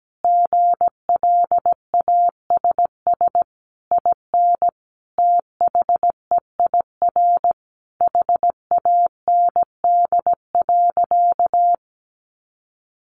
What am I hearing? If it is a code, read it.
Morse code, heard as a keyed tone, about 17 wpm, 700 Hz. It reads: GLASS IN THEIR HAND.
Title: Sentences from Top 1300 Words 17wpm